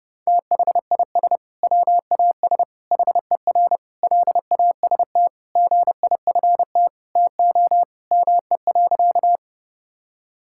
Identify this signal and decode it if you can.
Morse code, heard as a keyed tone, about 30 wpm, 700 Hz. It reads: THIS WAS HER LAST GIFT TO ME.
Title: Sentences from Top 1000 Words - Review Part 6 - 30wpm